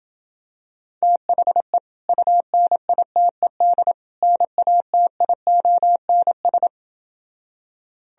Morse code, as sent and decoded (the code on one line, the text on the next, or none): - .... / ..- -. .. - . -.. / -. .- - .. --- -. ...
THE UNITED NATIONS